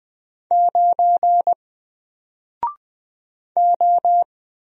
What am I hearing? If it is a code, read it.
Morse code, heard as a keyed tone, about 20 wpm, 700 Hz. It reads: 9 O
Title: Single Letter-Number Alt 20wpm